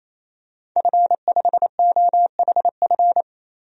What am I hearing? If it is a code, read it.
Morse code, heard as a keyed tone, about 28 wpm, 700 Hz. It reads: F5OHF
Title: DXCC-100 Call Signs - 5 Characters - 28wpm